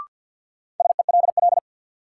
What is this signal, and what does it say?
Morse code, heard as a keyed tone, about 50 wpm, 700 Hz. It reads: SELL
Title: Sentences from Top 1100 Words - New Word Drill Alt - 50wpm